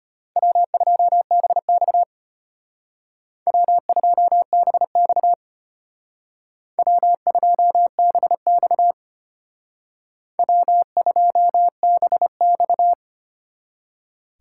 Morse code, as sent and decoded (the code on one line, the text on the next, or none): .-- ..--- -... -..- / .-- ..--- -... -..- / .-- ..--- -... -..- / .-- ..--- -... -..-
W2BX W2BX W2BX W2BX